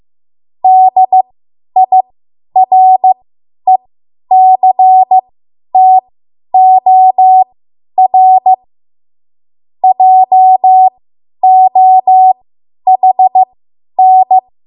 Morse code, heard as a keyed tone, about 15 wpm, spaced as for 10 wpm, 750 Hz. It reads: DIRECTOR JOHN
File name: Morse / 10WPM